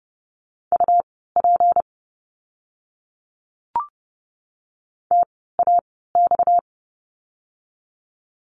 Morse code, heard as a keyed tone, about 30 wpm, spaced as for 17 wpm, 700 Hz. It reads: UP TAX
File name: course / 30:10wpm / 230 Lesson 37f - Letter x - Words 30wpm